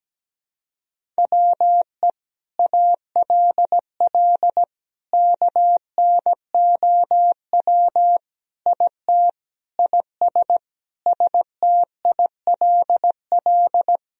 Morse code, heard as a keyed tone, about 17 wpm, 700 Hz. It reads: WE ALL KNOW IT IS STILL
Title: Sentences from Top 1000 Words - Review Part 1 - 17wpm